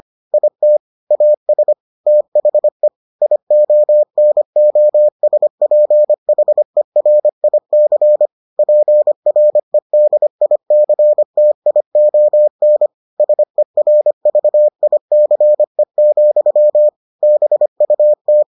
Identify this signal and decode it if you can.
Morse code, heard as a keyed tone, about 25 wpm, 600 Hz. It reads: IT AS THE IONOSPHERIC PREDICTION SERVICE, BUT